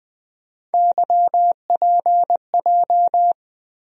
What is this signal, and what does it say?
Morse code, heard as a keyed tone, about 20 wpm, 700 Hz. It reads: YPJ
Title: USA Call Sign Postfixes - 3 Letters 20wpm